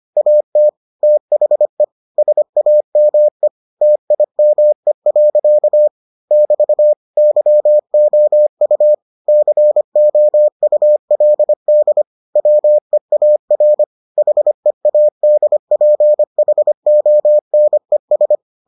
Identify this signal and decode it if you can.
Morse code, heard as a keyed tone, about 25 wpm, 600 Hz. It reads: AT THE SAME TIME. <BT> YOU COULD WEAR HEADPHONES